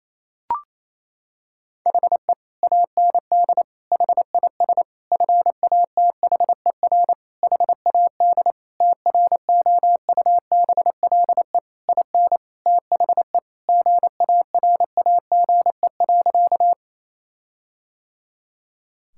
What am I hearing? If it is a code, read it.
Morse code, heard as a keyed tone, about 27 wpm, 700 Hz. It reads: HE AND HIS FATHER HAD TROUBLE IN THE GARAGE.
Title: Sentences from Top 1600 Words 28wpm